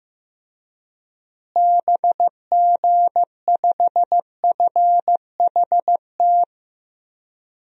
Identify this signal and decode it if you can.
Morse code, heard as a keyed tone, about 15 wpm, 700 Hz. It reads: BG5FHT
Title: DXCC-100 Call Signs - 6 Characters - Repeated 3x - 15wpm